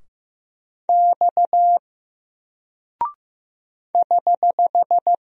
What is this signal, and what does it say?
Morse code, heard as a keyed tone, about 15 wpm, 700 Hz. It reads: X <HH>
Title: Single Character (Letter:Number:Punctuation:Prosign) 15wpm